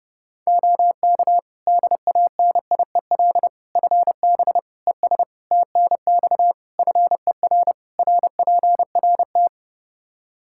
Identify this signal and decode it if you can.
Morse code, heard as a keyed tone, about 29 wpm, 700 Hz. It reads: OK DANIEL FB ES TNX FER RPRT
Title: Practice QSO - Exchange 2 - Info - 30wpm